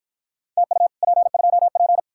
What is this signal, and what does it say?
Morse code, heard as a keyed tone, about 49 wpm, 700 Hz. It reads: TU P2P